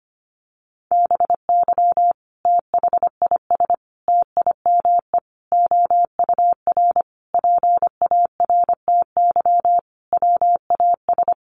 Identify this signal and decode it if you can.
Morse code, heard as a keyed tone, about 25 wpm, 700 Hz. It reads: BY THIS TIME OUR PARTY WAS
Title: Sentences from Top 1400 Words 25wpm